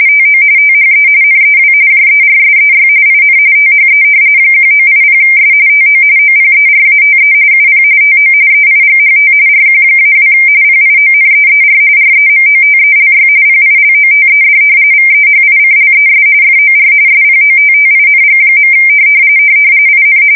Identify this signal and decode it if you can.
RTTY, about 45.45 baud. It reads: TO THE NEW JERSEY IN THE 1980S TO OPERATE UP TO EIGHT RQ-2 PIONEER REMOTELY PILOTED VEHICLES FOR SPOTTING. IN
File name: fsk170 / battleshipnj